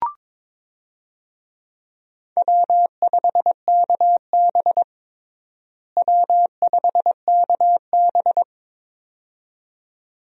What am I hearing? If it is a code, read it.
Morse code, heard as a keyed tone, about 22 wpm, 700 Hz. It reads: W5KB W5KB
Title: USA Call Signs - 4 Characters - Repeated 2x - 22wpm